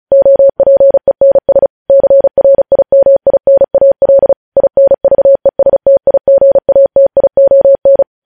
Morse code, heard as a keyed tone, about 36 wpm, 550 Hz. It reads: OPENS CRIMINAL INVESTIGATION
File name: podcast / News headlines-1x-35x35